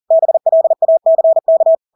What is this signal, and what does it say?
Morse code, heard as a keyed tone, about 40 wpm, 650 Hz. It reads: BLACK
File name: Engelske ord 40-40 wpm